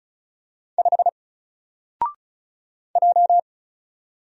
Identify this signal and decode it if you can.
Morse code, heard as a keyed tone, about 35 wpm, 700 Hz. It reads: 5 J
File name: Single-Letter-Number-35wpm